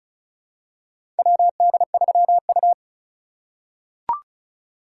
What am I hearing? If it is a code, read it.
Morse code, heard as a keyed tone, about 34 wpm, 700 Hz. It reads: WD3U
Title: USA Call Signs - 4 Characters - 2x Character Spacing - 35wpm